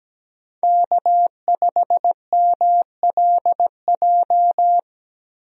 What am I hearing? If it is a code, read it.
Morse code, heard as a keyed tone, about 17 wpm, 700 Hz. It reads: K5MLJ
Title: USA Call Signs - 5 Characters - Repeated 2x - 17wpm